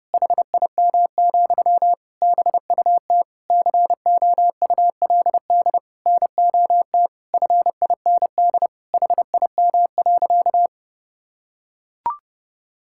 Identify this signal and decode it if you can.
Morse code, heard as a keyed tone, about 32 wpm, 700 Hz. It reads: HIM, BUT COULD NOT FIND HIM.
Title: Sentences from Top 1000 Words - Review Part 5 - 30wpm